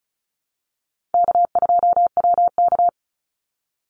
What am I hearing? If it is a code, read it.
Morse code, heard as a keyed tone, about 34 wpm, 700 Hz. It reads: K2WK
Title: Call Signs - 35wpm